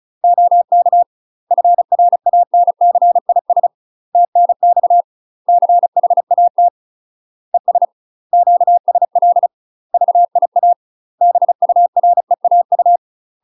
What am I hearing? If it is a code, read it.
Morse code, heard as a keyed tone, about 35 wpm, 700 Hz. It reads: OK FRANCIS TNX CHAT ES QSL VIA BUREAU
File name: UK MASTER QSOs TRAINING / 35WPM / UK MASTER QSOs 0023